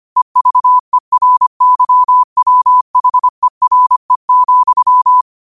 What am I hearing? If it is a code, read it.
Morse code, heard as a keyed tone, about 25 wpm, 1000 Hz. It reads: EVERYWHERE,
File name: qotdfiles / 25wpm